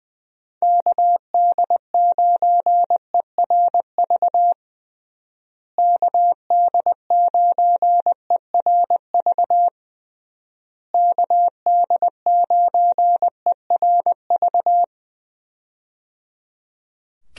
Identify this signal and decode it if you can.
Morse code, heard as a keyed tone, about 20 wpm, 700 Hz. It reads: KD9ERV KD9ERV KD9ERV
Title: USA Call Signs - 6 Characters - Repeated 3x - 20wpm